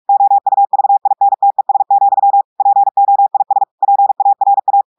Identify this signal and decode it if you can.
Morse code, heard as a keyed tone, about 46 wpm, 800 Hz. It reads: OUVINTES, POIS PARA
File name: audios / portugais / portugais texte en clair 2 vitesse 45 mots minutes